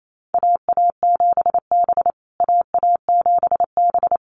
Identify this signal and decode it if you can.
Morse code, heard as a keyed tone, about 28 wpm, 700 Hz. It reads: AA7B AA7B
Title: USA Call Signs Repeated 2x 28wpm